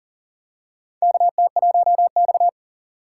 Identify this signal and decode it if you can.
Morse code, heard as a keyed tone, about 40 wpm, 700 Hz. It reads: KT1X
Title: USA Call Signs - 4 Characters - 6x Character Spacing - 40wpm